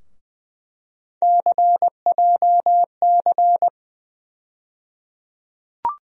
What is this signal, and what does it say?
Morse code, heard as a keyed tone, about 20 wpm, 700 Hz. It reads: CJC
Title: USA Call Sign Postfixes - 3 Letters 20wpm